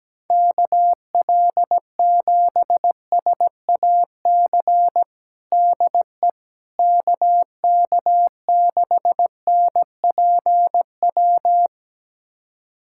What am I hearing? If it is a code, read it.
Morse code, heard as a keyed tone, about 17 wpm, 700 Hz. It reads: KL7SAC DE KK6NPW
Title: USA Callsign DE Callsign - 6 Characters - 17wpm